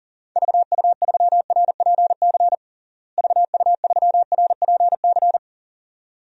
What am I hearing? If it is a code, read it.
Morse code, heard as a keyed tone, about 40 wpm, 700 Hz. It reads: VU3RPC VU3RPC
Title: DXCC-100 Call Signs - 6 Characters - Repeated 2x - 40wpm